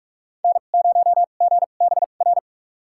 Morse code, heard as a keyed tone, about 44 wpm, spaced as for 32 wpm, 700 Hz. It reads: N0GBR
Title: USA Call Signs - 5 Characters - 2x Character Spacing - 45wpm